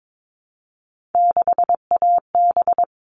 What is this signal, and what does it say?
Morse code, heard as a keyed tone, about 22 wpm, 700 Hz. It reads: 6AB
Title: All Cut Numbers - 3-Digit 22wpm